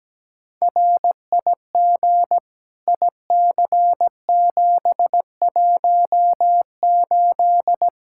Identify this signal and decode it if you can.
Morse code, heard as a keyed tone, about 17 wpm, 700 Hz. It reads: RIG IC718
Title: Practice QSO - Exchange 2 - Info - 17wpm